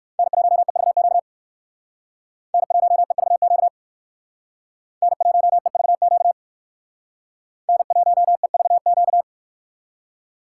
Morse code, heard as a keyed tone, about 63 wpm, 700 Hz. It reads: N1EVQ N1EVQ N1EVQ N1EVQ